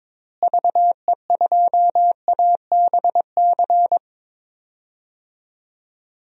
Morse code, heard as a keyed tone, about 22 wpm, 700 Hz. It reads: VE2ABC
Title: DXCC-100 Call Signs - 6 Characters - Repeated 3x - 22wpm